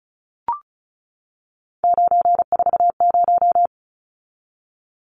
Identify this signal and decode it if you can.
Morse code, heard as a keyed tone, about 34 wpm, 700 Hz. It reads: 940
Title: Numbers - 3-Digit 35wpm